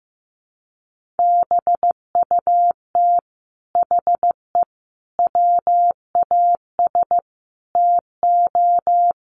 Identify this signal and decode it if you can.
Morse code, heard as a keyed tone, about 15 wpm, 700 Hz. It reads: BUT HE WAS TO
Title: Sentences from Top 1500 Words 15wpm